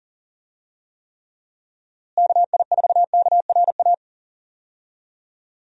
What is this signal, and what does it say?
Morse code, heard as a keyed tone, about 40 wpm, 700 Hz. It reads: KI4KRA
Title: USA Call Signs - 6 Characters - Repeated 3x - 40wpm